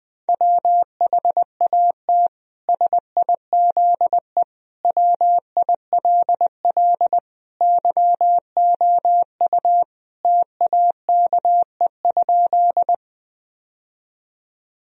Morse code, heard as a keyed tone, about 20 wpm, 700 Hz. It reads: WHAT SIZE WILL YOU TAKE?
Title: Sentences from Top 1000 Words - Review Part 2 - 20wpm